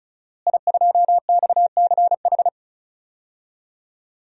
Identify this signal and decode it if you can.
Morse code, heard as a keyed tone, about 36 wpm, 700 Hz. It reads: I2XCH
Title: DXCC-100 Call Signs - 5 Characters - Repeated 3x - 35wpm